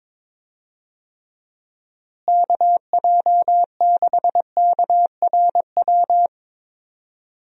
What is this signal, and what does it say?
Morse code, heard as a keyed tone, about 22 wpm, 700 Hz. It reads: KJ6KRW
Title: USA Call Signs - 6 Characters - Repeated 3x - 22wpm